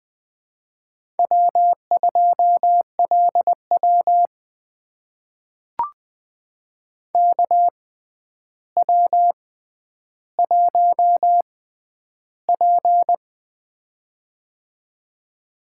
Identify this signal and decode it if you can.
Morse code, heard as a keyed tone, about 20 wpm, spaced as for 10 wpm, 700 Hz. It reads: W2LW KW1P
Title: USA Call Signs - 4 Characters - 6x Character Spacing - 20wpm